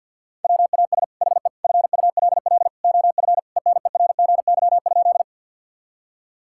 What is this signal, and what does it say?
Morse code, heard as a keyed tone, about 50 wpm, 700 Hz. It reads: WAS HE FULL OF ENERGY?
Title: Sentences from Top 1000 Words - Review Part 6 - 50wpm